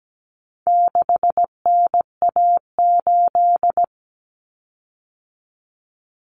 Morse code, heard as a keyed tone, about 17 wpm, 700 Hz. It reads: 6NA8
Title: Common Cut Numbers - 4-Digit 17wpm